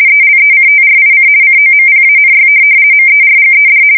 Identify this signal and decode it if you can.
RTTY, about 50 baud. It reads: ELECTRIC COMPANY MERGED
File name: fsk170-50B / teletype-corp